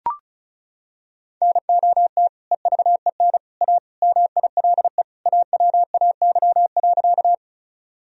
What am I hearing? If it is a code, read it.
Morse code, heard as a keyed tone, about 36 wpm, 700 Hz. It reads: NOT EVEN A MILE AWAY.
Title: Sentences from Top 1000 Words - Review Part 1 - 35wpm